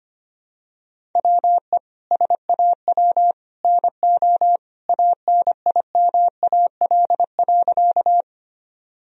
Morse code, heard as a keyed tone, about 25 wpm, 700 Hz. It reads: WE SAW NO ANIMAL.
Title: Sentences from Top 1000 Words - Review Part 1 - 25wpm